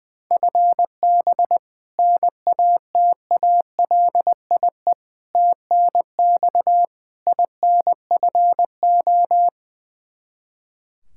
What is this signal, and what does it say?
Morse code, heard as a keyed tone, about 20 wpm, 700 Hz. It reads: FB NATALIE TNX INFO
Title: Practice QSO - Exchange 3 - More Info - 20wpm